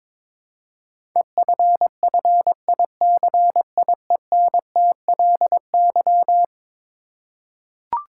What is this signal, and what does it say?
Morse code, heard as a keyed tone, about 21 wpm, 700 Hz. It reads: EFFICIENTLY
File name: practice / 22wpm / Eleven Letter Words - 22wpm